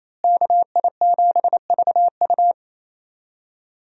Morse code, heard as a keyed tone, about 27 wpm, 700 Hz. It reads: KI7VU
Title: USA Call Signs - 5 Characters - Repeated 2x - 28wpm